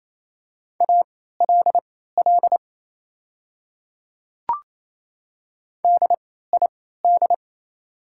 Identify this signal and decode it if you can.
Morse code, heard as a keyed tone, about 28 wpm, spaced as for 16 wpm, 690 Hz. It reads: ALL DID
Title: 3x Character Spacing - Top 100 Words - 28wpm